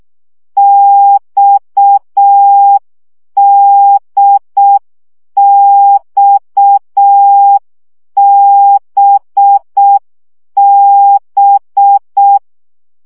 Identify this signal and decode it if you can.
Morse code, heard as a keyed tone, about 6 wpm, 800 Hz. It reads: XDXBB
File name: CW gyakorlas d b x 5 baud